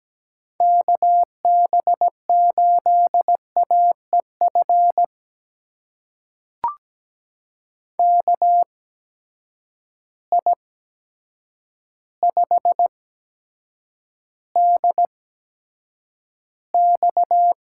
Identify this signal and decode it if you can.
Morse code, heard as a keyed tone, about 17 wpm, spaced as for 8 wpm, 700 Hz. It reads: KB8AEF KI5DX
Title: USA Call Signs - 6 Characters - 8x Character Spacing - 17wpm